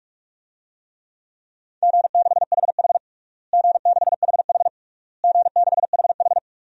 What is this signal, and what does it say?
Morse code, heard as a keyed tone, about 46 wpm, 700 Hz. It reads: G6HH G6HH G6HH
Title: DXCC-100 Call Signs - 4 Characters - Repeated 3x - 45wpm